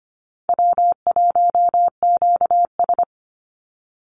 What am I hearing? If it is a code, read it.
Morse code, heard as a keyed tone, about 25 wpm, 700 Hz. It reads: W1QS
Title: Easy Call Signs Repeated 3x - 25wpm